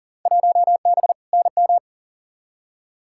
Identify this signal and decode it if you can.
Morse code, heard as a keyed tone, about 40 wpm, 700 Hz. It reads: 1B NM
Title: Field Day Exchanges Encore - 40wpm